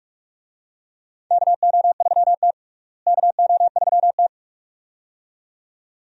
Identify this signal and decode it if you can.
Morse code, heard as a keyed tone, about 46 wpm, 700 Hz. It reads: KO3T KO3T
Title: USA Call Signs - 4 Characters - Repeated 2x - 45wpm